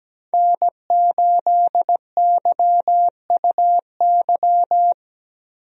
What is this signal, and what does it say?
Morse code, heard as a keyed tone, about 17 wpm, 700 Hz. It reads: N8YUY
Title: USA Call Signs - 5 Characters - 17wpm